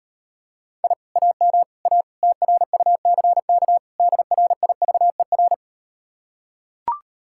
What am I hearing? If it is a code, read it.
Morse code, heard as a keyed tone, about 38 wpm, 700 Hz. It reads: I AM A TRUCK DRIVER